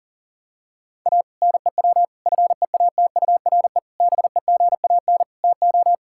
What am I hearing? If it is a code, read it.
Morse code, heard as a keyed tone, about 40 wpm, 700 Hz. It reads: A NEW FEATURE BEGAN TO
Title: Sentences from Top 1000 Words - Review Part 5 - 40wpm